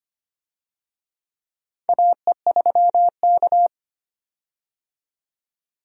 Morse code, heard as a keyed tone, about 25 wpm, 700 Hz. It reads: AE3K
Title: USA Call Signs - 4 Characters - 25wpm